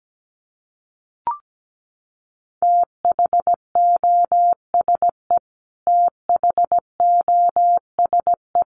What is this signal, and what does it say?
Morse code, heard as a keyed tone, about 17 wpm, 700 Hz. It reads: THOSE THOSE
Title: Top 300 Words Repeated 17wpm